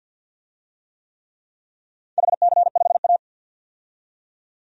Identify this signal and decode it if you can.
Morse code, heard as a keyed tone, about 50 wpm, 700 Hz. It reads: HC5A
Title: DXCC-100 Call Signs - 4 Characters - 50wpm